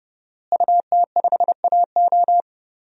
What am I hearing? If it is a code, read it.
Morse code, heard as a keyed tone, about 30 wpm, 700 Hz. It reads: UT5AO
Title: DXCC-100 Call Signs - 5 Characters - Repeated 3x - 30wpm